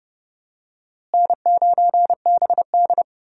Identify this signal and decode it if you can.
Morse code, heard as a keyed tone, about 30 wpm, 700 Hz. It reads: N9BD
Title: USA Call Signs - 4 Characters - Repeated 2x - 30wpm